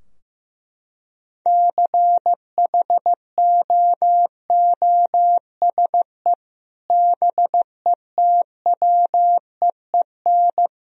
Morse code, heard as a keyed tone, about 15 wpm, 700 Hz. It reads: CHOOSE BETWEEN
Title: Sentences from Top 1000 Words - Review Part 4 - 15wpm